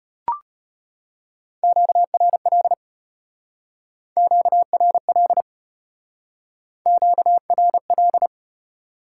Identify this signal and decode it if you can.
Morse code, heard as a keyed tone, about 35 wpm, 700 Hz. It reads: QRL QRL QRL